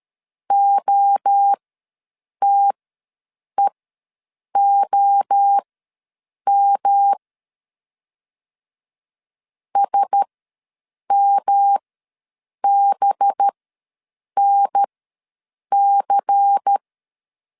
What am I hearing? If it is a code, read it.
Morse code, heard as a keyed tone, about 13 wpm, spaced as for 7 wpm, 800 Hz. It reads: OTEOM SMBNC
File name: LEZIONI / 7 GF9Z / 7 GF9 ws 12wpm 3344